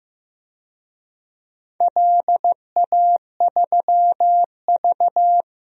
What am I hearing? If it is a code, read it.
Morse code, heard as a keyed tone, about 15 wpm, 700 Hz. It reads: LA3V
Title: DXCC-100 Call Signs - 4 Characters - Repeated 3x - 15wpm